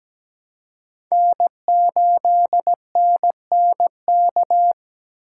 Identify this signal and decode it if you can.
Morse code, heard as a keyed tone, about 17 wpm, 700 Hz. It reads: N8NNK
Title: USA Call Signs - 2x Character Spacing - 17wpm